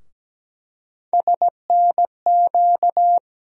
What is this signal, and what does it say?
Morse code, heard as a keyed tone, about 17 wpm, 700 Hz. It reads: SNQ
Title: USA Call Sign Postfixes - 3 Letters 17wpm